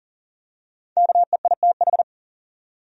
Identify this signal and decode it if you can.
Morse code, heard as a keyed tone, about 40 wpm, 700 Hz. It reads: KEITH
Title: USA Common Names - Max 5 Letters - 40wpm